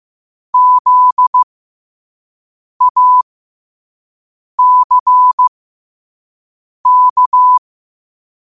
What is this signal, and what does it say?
Morse code, heard as a keyed tone, about 15 wpm, spaced as for 5 wpm, 1000 Hz. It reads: ZACK
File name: qotdfiles / 5wpm